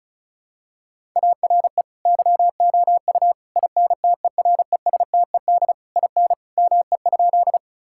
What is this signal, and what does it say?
Morse code, heard as a keyed tone, about 36 wpm, 700 Hz. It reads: ARE YOU INTERESTED IN ME?
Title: Sentences from Top 1000 Words - Review Part 6 - 35wpm